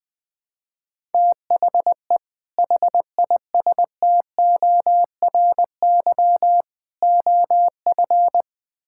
Morse code, heard as a keyed tone, about 21 wpm, 700 Hz. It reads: THE HISTORY OF
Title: Build A Phrase - 3-Word Phrases - 20wpm